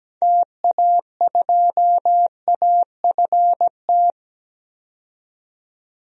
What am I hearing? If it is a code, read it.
Morse code, heard as a keyed tone, about 17 wpm, 700 Hz. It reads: TA2AFT
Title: DXCC-100 Call Signs - 6 Characters - 17wpm